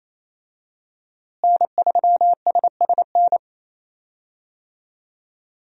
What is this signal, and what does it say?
Morse code, heard as a keyed tone, about 28 wpm, 700 Hz. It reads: N3SSN
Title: USA Call Signs - 5 Characters - Repeated 3x - 28wpm